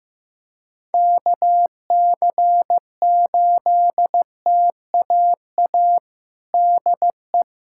Morse code, heard as a keyed tone, about 15 wpm, 700 Hz. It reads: KC8TAA DE
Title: USA Callsign DE Callsign - 6 Characters - 15wpm